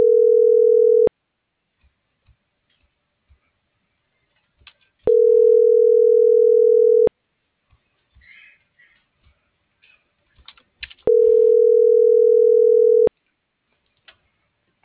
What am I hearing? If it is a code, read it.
Ringback tone (North America), 440 + 480 Hz.